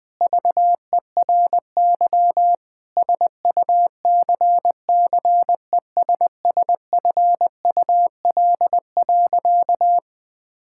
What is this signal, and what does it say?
Morse code, heard as a keyed tone, about 20 wpm, 700 Hz. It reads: VERY SUCCESSFUL.
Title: Sentences from Top 1000 Words - Review Part 6 - 20wpm